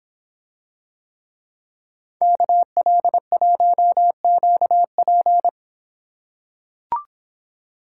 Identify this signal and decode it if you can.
Morse code, heard as a keyed tone, about 26 wpm, 700 Hz. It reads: KL1QP